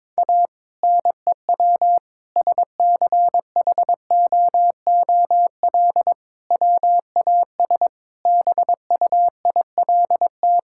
Morse code, heard as a keyed tone, about 22 wpm, 700 Hz. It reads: A NEW SCHOOL WAS BUILT
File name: practice / 22wpm / Sentences with 2-Word Phrases - 22wpm